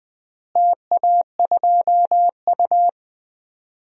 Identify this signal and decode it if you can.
Morse code, heard as a keyed tone, about 20 wpm, 700 Hz. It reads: TA2U
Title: DXCC-100 Call Signs - 4 Characters - Repeated 3x - 20wpm